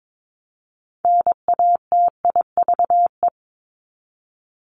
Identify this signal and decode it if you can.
Morse code, heard as a keyed tone, about 22 wpm, 700 Hz. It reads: NATIVE
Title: Top 1751-2000 Words 22wpm